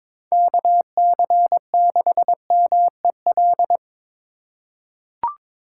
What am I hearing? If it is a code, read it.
Morse code, heard as a keyed tone, about 22 wpm, 700 Hz. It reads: KC6MEL
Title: USA Call Signs - 4x Character Spacing - 22wpm